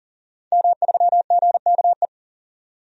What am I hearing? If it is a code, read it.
Morse code, heard as a keyed tone, about 40 wpm, 700 Hz. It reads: M3GKE
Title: DXCC-100 Call Signs - 5 Characters - 40wpm